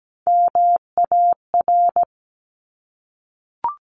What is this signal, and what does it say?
Morse code, heard as a keyed tone, about 17 wpm, 700 Hz. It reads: MAR
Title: ARRL Sections 17wpm